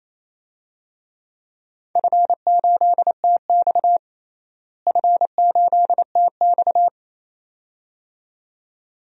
Morse code, heard as a keyed tone, about 27 wpm, 700 Hz. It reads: F8TX F8TX
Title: DXCC-100 Call Signs - 4 Characters - Repeated 2x - 28wpm